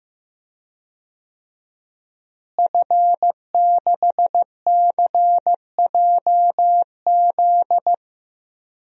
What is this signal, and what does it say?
Morse code, heard as a keyed tone, about 15 wpm, 700 Hz. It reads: F6CJZ
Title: DXCC-100 Call Signs - 5 Characters - Repeated 3x - 15wpm